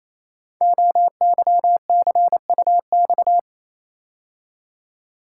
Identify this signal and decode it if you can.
Morse code, heard as a keyed tone, about 27 wpm, 700 Hz. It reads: OYCUX
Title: Code Groups - 5 Letter:Numbers - 28wpm